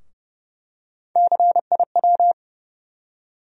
Morse code, heard as a keyed tone, about 31 wpm, 700 Hz. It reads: CIW
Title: USA Call Sign Postfixes - 3 Letters 30wpm